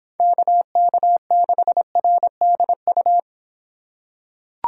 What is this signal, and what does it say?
Morse code, heard as a keyed tone, about 25 wpm, 700 Hz. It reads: KK6RDU